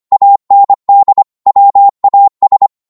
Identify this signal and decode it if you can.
Morse code, heard as a keyed tone, about 25 wpm, 800 Hz. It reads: AND WAS
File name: Morse / psg-25wpm